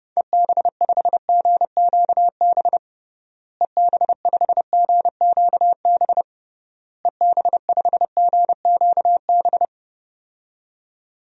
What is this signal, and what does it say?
Morse code, heard as a keyed tone, about 30 wpm, 700 Hz. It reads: EB5GQB EB5GQB EB5GQB
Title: DXCC-100 Call Signs - 6 Characters - Repeated 3x - 30wpm